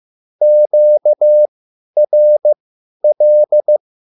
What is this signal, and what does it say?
Morse code, heard as a keyed tone, about 15 wpm, spaced as for 10 wpm, 600 Hz. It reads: QRL
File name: QSL / CW lesson